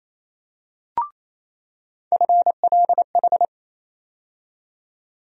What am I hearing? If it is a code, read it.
Morse code, heard as a keyed tone, about 28 wpm, 700 Hz. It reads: FLH
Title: USA Call Sign Postfixes - 3 Letters 28wpm